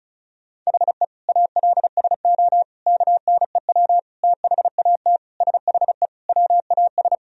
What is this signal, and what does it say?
Morse code, heard as a keyed tone, about 36 wpm, 700 Hz. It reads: HE ALSO KNEW THAT SHE WAS
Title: Sentences from Top 1000 Words - Review Part 1 - 35wpm